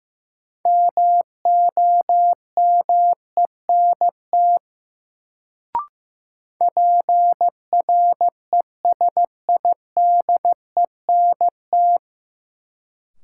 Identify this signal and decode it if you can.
Morse code, heard as a keyed tone, about 15 wpm, 700 Hz. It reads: MOMENT PRESIDENT
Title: Sentences from Top 1000 Words - New Word Drill - 15wpm